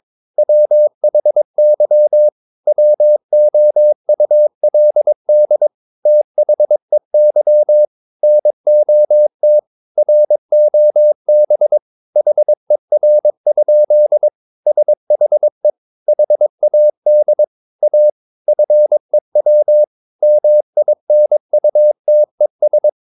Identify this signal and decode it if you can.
Morse code, heard as a keyed tone, about 22 wpm, 600 Hz. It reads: WHY WOULD THEY NOT ROB HER? SHE HAD A FEW MINUTES
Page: CW med Gnister